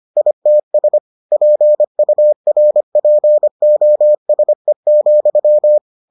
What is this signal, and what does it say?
Morse code, heard as a keyed tone, about 25 wpm, 600 Hz. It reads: ITS PURPOSE,